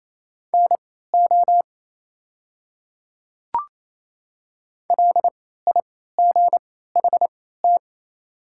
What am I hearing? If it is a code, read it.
Morse code, heard as a keyed tone, about 28 wpm, spaced as for 16 wpm, 700 Hz. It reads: NO LIGHT
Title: 3x Character Spacing - Top 200 Words - 28wpm